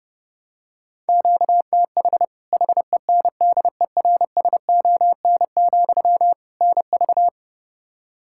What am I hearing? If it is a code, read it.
Morse code, heard as a keyed tone, about 30 wpm, 700 Hz. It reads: QTH HENDERSON, NV
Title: Practice QSO - Exchange 1 - Beginning - 30wpm